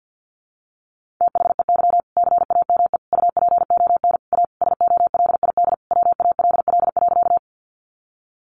Morse code, heard as a keyed tone, about 50 wpm, 700 Hz. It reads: THEY CAME UPON A SOLID WALL.
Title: Sentences from Top 1500 Words 50wpm